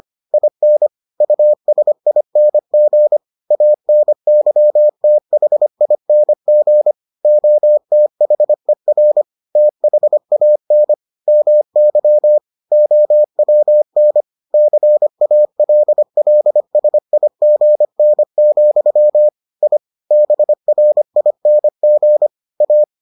Morse code, heard as a keyed tone, about 25 wpm, 600 Hz. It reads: IN USING ANYTHING OTHER THAN MY OWN CALLSIGN, I BRING A